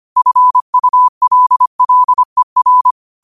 Morse code, heard as a keyed tone, about 25 wpm, 1000 Hz. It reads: FULLER
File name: qotdfiles / 25wpm